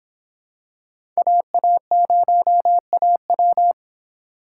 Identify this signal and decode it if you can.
Morse code, heard as a keyed tone, about 26 wpm, 700 Hz. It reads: AA0AW